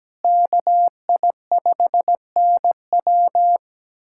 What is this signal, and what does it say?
Morse code, heard as a keyed tone, about 17 wpm, 700 Hz. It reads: KI5NW
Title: USA Callsign DE Callsign 17wpm